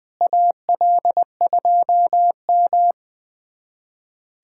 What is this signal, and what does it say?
Morse code, heard as a keyed tone, about 20 wpm, 700 Hz. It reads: AL2M
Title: USA Callsign DE Callsign 20wpm